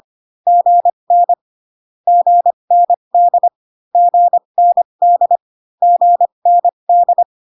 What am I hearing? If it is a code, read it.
Morse code, heard as a keyed tone, about 25 wpm, spaced as for 22 wpm, 700 Hz. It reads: GN GND GND GND
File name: CW common abbreviations alphabetical repeated 3 times 25 18wpm